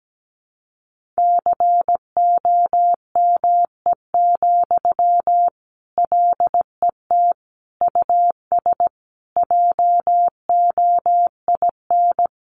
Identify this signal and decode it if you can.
Morse code, heard as a keyed tone, about 17 wpm, 700 Hz. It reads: COME, LET US JOIN
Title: Sentences from Top 1400 Words 17wpm